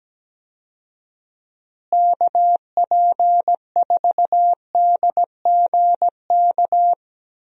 Morse code, heard as a keyed tone, about 17 wpm, 700 Hz. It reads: KP4DGK
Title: USA Call Signs - 6 Characters - Repeated 2x - 17wpm